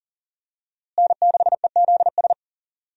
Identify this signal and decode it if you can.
Morse code, heard as a keyed tone, about 40 wpm, 700 Hz. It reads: N6EZS
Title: USA Call Signs - 5 Characters - Repeated 2x - 40wpm